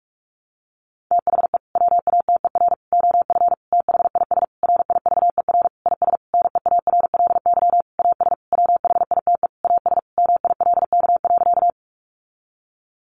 Morse code, heard as a keyed tone, about 44 wpm, 700 Hz. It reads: THE WATER OF THIS RIVER IS NEARLY AS WHITE AS MILK.
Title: Sentences from Top 1500 Words 45wpm